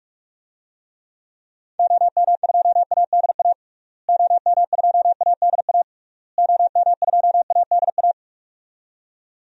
Morse code, heard as a keyed tone, about 41 wpm, 700 Hz. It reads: OM2ADA OM2ADA OM2ADA
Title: DXCC-100 Call Signs - 6 Characters - Repeated 3x - 45wpm